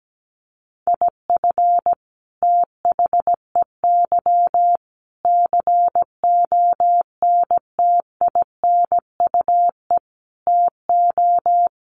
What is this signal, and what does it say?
Morse code, heard as a keyed tone, about 17 wpm, 700 Hz. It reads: IF THEY CONTINUE TO
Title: Sentences from Top 1400 Words 17wpm